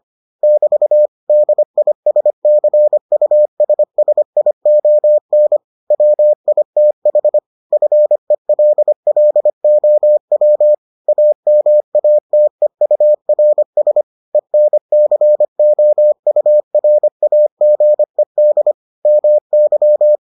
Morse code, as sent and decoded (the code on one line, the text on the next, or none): -...- / -.. .. ... -.-. ..- ... ... .. --- -. / .-- .. - .... / ..-. .-.. .-.. --- .-- / .- -- .- - . ..- .-. ... / . -. -.-. --- ..- .-. .- --. -.. / -- -.--
<BT> DISCUSSION WITH FELLOW AMATEURS ENCOURAGED MY